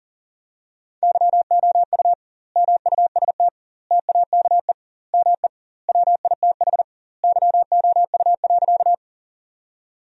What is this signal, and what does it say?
Morse code, heard as a keyed tone, about 40 wpm, 700 Hz. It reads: YOU MUST TAKE ME WITH YOU.
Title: 2x Word-Spacing - Sentences from Top 200 Words - 40wpm